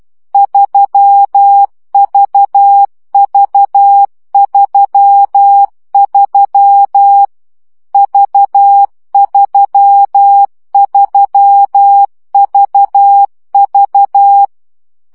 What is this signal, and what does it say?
Morse code, heard as a keyed tone, about 12 wpm, 800 Hz. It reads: 3VV33 V33VV
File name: CW gyakorlas v 3 10 baud